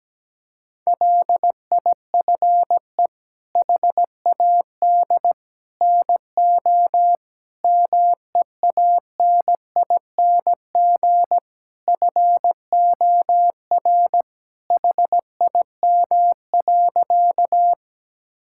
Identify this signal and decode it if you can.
Morse code, heard as a keyed tone, about 17 wpm, 700 Hz. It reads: LIFE HAD NO MEANING FOR HIM.
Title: Sentences from Top 1000 Words - Review Part 6 - 17wpm